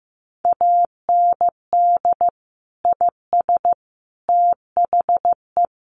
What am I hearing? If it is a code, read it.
Morse code, heard as a keyed tone, about 15 wpm, 700 Hz. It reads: AND IS THE
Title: Sentences from Top 700 Words 15wpm